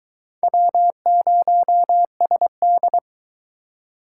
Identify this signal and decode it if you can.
Morse code, heard as a keyed tone, about 23 wpm, 700 Hz. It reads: W0SD